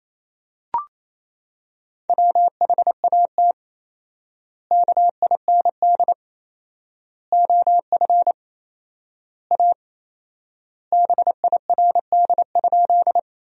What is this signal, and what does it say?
Morse code, heard as a keyed tone, about 28 wpm, 700 Hz. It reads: WHAT KIND OF A BIRD?
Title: 4x Word-Spacing - Sentences from Top 400 Words - 28wpm